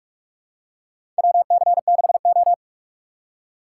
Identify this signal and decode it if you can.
Morse code, heard as a keyed tone, about 46 wpm, 700 Hz. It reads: WC6O
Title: USA Call Signs - 4 Characters - 3x Character Spacing - 45wpm